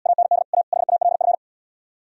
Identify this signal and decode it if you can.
Morse code, heard as a keyed tone, about 77 wpm, 700 Hz. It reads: AND A HALF